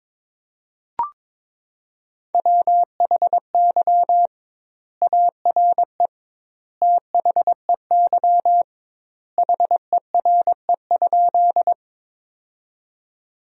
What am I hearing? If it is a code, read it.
Morse code, heard as a keyed tone, about 21 wpm, 700 Hz. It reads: WHY ARE THEY HERE?
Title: 2x Word-Spacing - Sentences from Top 200 Words - 22wpm